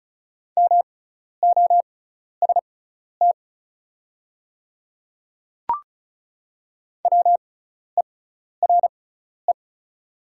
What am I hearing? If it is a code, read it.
Morse code, heard as a keyed tone, about 35 wpm, spaced as for 12 wpm, 690 Hz. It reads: MOST WERE
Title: 6x Character Spacing - Top 100 Words - 35wpm